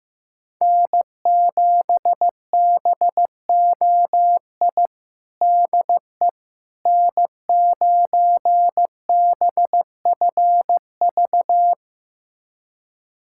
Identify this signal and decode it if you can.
Morse code, heard as a keyed tone, about 16 wpm, 700 Hz. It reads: N7BOI DE N9BFV
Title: USA Callsign DE Callsign - 5 Characters - 15wpm